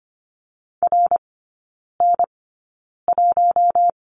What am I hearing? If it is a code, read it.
Morse code, heard as a keyed tone, about 25 wpm, spaced as for 9 wpm, 690 Hz. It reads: RN1
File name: Lesson 15e - Number 5 - Three Characters 25wpm